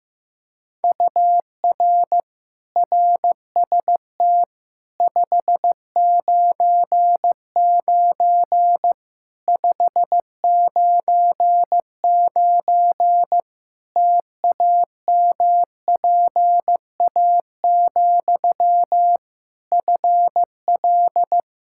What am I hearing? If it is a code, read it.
Morse code, heard as a keyed tone, about 15 wpm, 700 Hz. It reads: UR RST 599 599 TAMPA, FL
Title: K1USN SST QSO Practice 15wpm